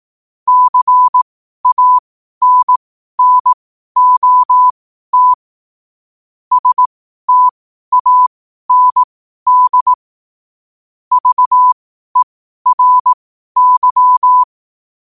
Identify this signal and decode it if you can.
Morse code, heard as a keyed tone, about 18 wpm, spaced as for 12 wpm, 1000 Hz. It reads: CANNOT STAND VERY
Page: Quotes for Thu, 14 Aug 2025 in Morse Code at 12 words per minute.